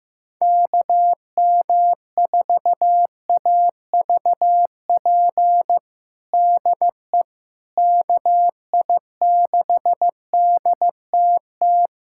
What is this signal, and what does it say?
Morse code, heard as a keyed tone, about 15 wpm, 700 Hz. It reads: KM4AVP DE KI6DTT
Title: USA Callsign DE Callsign - 6 Characters - 15wpm